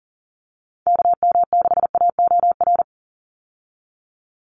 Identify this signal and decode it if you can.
Morse code, heard as a keyed tone, about 41 wpm, 700 Hz. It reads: KM6AOR
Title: USA Call Signs 40wpm